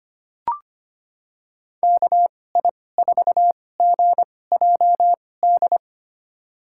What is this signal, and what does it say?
Morse code, heard as a keyed tone, about 25 wpm, spaced as for 18 wpm, 700 Hz. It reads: KI4GJD
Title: USA Call Signs - 6 Characters - 2x Character Spacing - 25wpm